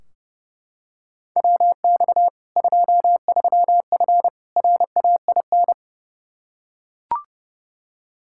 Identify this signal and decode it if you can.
Morse code, heard as a keyed tone, about 30 wpm, 700 Hz. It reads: WX 23F RAIN
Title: Practice QSO - Exchange 2 - Info - 30wpm